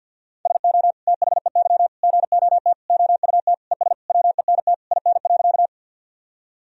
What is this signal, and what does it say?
Morse code, heard as a keyed tone, about 48 wpm, 700 Hz. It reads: SO THEY GOT OUT ES WENT IN.
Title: Sentences from Top 300 Words and Abbreviations 50wpm